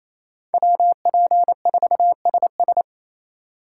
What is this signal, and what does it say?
Morse code, heard as a keyed tone, about 28 wpm, 700 Hz. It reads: WP4SS
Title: USA Call Signs - 5 Characters - 8x Character Spacing - 28wpm